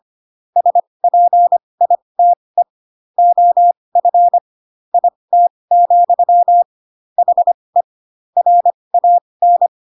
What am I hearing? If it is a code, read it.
Morse code, heard as a keyed tone, about 25 wpm, spaced as for 20 wpm, 700 Hz. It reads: SPITE OF IT, HE RAN
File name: Pinocchio / Pinocchio ch6 15 25 ik1hge